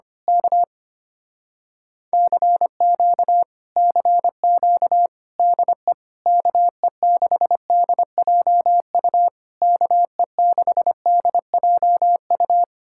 Morse code, heard as a keyed tone, about 24 wpm, 700 Hz. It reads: K CQ CQ DE KE6DJU KE6DJU
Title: Speed-Racing Common QSO Phrases - 25wpm